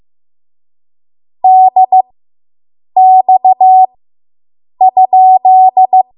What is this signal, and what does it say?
Morse code, heard as a keyed tone, about 15 wpm, spaced as for 7 wpm, 750 Hz. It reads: DX?